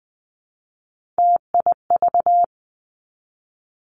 Morse code, heard as a keyed tone, about 20 wpm, 700 Hz. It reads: TIV
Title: Three Challenging Letter-Number Combinations 20wpm